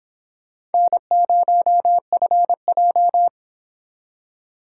Morse code, heard as a keyed tone, about 29 wpm, 700 Hz. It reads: N0FJ